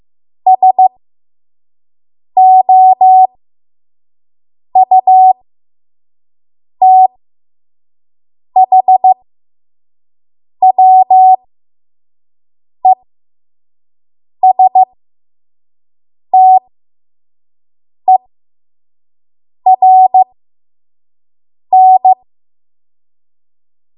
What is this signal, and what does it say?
Morse code, heard as a keyed tone, about 15 wpm, spaced as for 5 wpm, 750 Hz. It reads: SOUTHWESTERN
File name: Morse / 05WPM